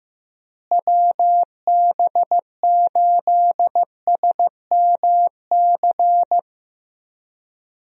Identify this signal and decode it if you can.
Morse code, heard as a keyed tone, about 16 wpm, 700 Hz. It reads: WB8SMC
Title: USA Call Signs - 6 Characters - Repeated 2x - 15wpm